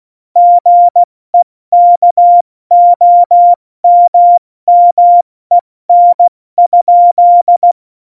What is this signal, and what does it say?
Morse code, heard as a keyed tone, about 16 wpm, 700 Hz. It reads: GEKOMMEN?
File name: Morsekurs / Witze / Witz 31. Oktober 17, Tempo 70